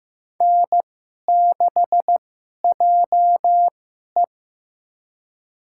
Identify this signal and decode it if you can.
Morse code, heard as a keyed tone, about 15 wpm, spaced as for 11 wpm, 700 Hz. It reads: N6JE
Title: USA Call Signs - 2x Character Spacing - 15wpm